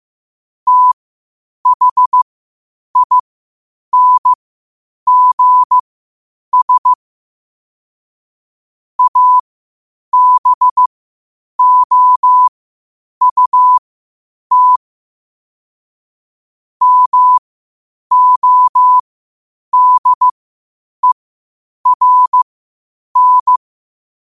Morse code, heard as a keyed tone, about 15 wpm, spaced as for 8 wpm, 1000 Hz. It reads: THINGS ABOUT MODERN
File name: qotdfiles / 8wpm